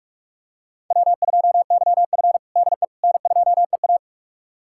Morse code, heard as a keyed tone, about 42 wpm, 700 Hz. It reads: W2YF DE N2EA
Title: USA Callsign DE Callsign - 4 Characters - 45wpm